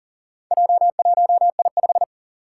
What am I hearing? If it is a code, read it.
Morse code, heard as a keyed tone, about 39 wpm, 700 Hz. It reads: J1I5
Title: Four Challenging Letter-Number Combinations 40wpm